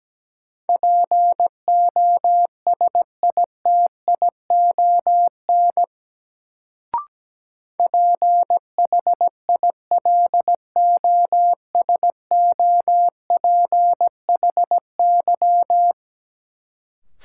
Morse code, heard as a keyed tone, about 17 wpm, 700 Hz. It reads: POSITION PHILOSOPHY
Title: Sentences from Top 900 Words - New Word Drill - 17wpm